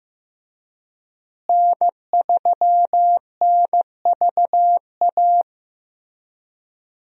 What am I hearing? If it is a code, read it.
Morse code, heard as a keyed tone, about 15 wpm, 700 Hz. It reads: N3NVA
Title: USA Call Signs - 5 Characters - Repeated 3x - 15wpm